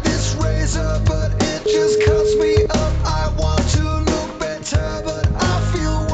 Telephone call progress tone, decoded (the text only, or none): ringback tone